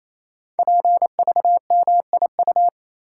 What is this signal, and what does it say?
Morse code, heard as a keyed tone, about 29 wpm, 700 Hz. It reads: PVMIU
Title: Code Groups - 5 Letter:Numbers - 28wpm